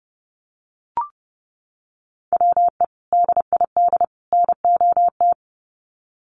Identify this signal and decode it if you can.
Morse code, heard as a keyed tone, about 30 wpm, 700 Hz. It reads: WE DID NOT
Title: Sets of 3 Words 30wpm